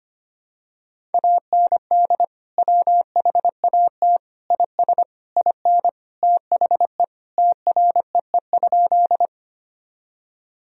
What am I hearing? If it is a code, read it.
Morse code, heard as a keyed tone, about 25 wpm, 700 Hz. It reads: AND WHAT IS IN THE TREE?
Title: Sentences from Top 1000 Words - Review Part 1 - 25wpm